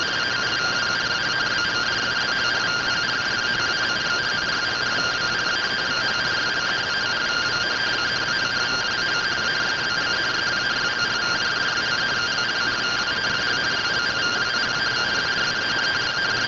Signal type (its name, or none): two-tone teleprinter signal